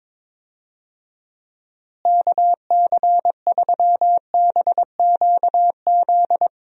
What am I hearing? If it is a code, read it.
Morse code, heard as a keyed tone, about 22 wpm, 700 Hz. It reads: KC3BQZ
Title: USA Call Signs - 6 Characters - Repeated 2x - 22wpm